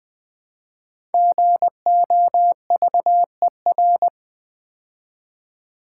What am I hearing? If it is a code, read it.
Morse code, heard as a keyed tone, about 20 wpm, 700 Hz. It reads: GOVER
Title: Build A Word - 5 Letters At A Time - 10-Letter Words - 20wpm